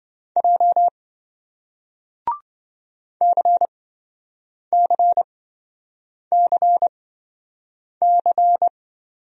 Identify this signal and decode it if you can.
Morse code, heard as a keyed tone, about 26 wpm, spaced as for 7 wpm, 700 Hz. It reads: J CCCC